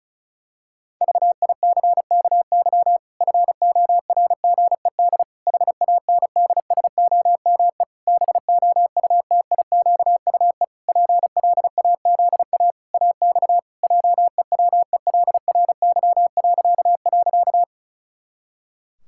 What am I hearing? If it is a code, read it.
Morse code, heard as a keyed tone, about 34 wpm, 700 Hz. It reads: VICKY FORGED HANDSOME BOUTIQUE PLAZA AX JEWELRY..
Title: Pangram Challenge v2 - 35wpm